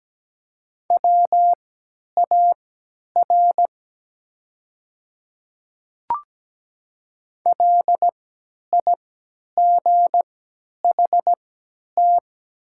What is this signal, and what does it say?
Morse code, heard as a keyed tone, about 17 wpm, spaced as for 10 wpm, 700 Hz. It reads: WAR LIGHT
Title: 3x Character Spacing - Top 500 Words - 17wpm